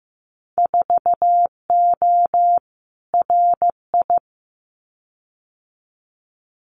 Morse code, heard as a keyed tone, about 15 wpm, 700 Hz. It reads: 4O RI
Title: Winter Field Day Exchanges Encore 15wpm